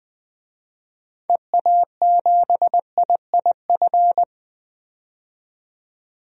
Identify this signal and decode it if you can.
Morse code, heard as a keyed tone, about 20 wpm, 700 Hz. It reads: EA7IIF
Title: DXCC-100 Call Signs - 6 Characters - 20wpm